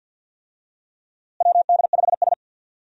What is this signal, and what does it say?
Morse code, heard as a keyed tone, about 46 wpm, 700 Hz. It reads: WD5S
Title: USA Call Signs - 4 Characters - Repeated 2x - 50wpm